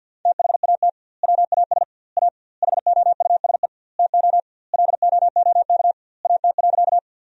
Morse code, heard as a keyed tone, about 50 wpm, 700 Hz. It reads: THAT WAS A HOUSE TO LOOK AT.
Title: 2x Word-Spacing - Sentences from Top 200 Words - 50wpm